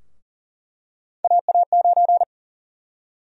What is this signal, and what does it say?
Morse code, heard as a keyed tone, about 40 wpm, 700 Hz. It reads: AA9
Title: USA Call Sign Prefixes - Repeated 2x - 40wpm